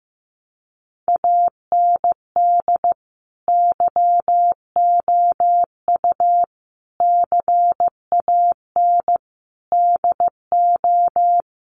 Morse code, heard as a keyed tone, about 15 wpm, 700 Hz. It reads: AND YOU CAN DO
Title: Sentences from Top 500 Words 15wpm